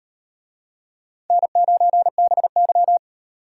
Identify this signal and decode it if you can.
Morse code, heard as a keyed tone, about 38 wpm, 700 Hz. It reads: N9BY